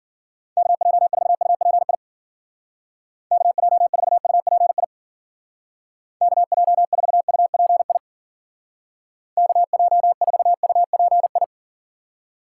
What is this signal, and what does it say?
Morse code, heard as a keyed tone, about 50 wpm, 700 Hz. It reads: KJ4UPI KJ4UPI KJ4UPI KJ4UPI